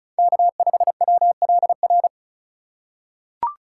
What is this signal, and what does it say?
Morse code, heard as a keyed tone, about 36 wpm, 700 Hz. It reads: K5WLR
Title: USA Call Signs - 5 Characters - 3x Character Spacing - 35wpm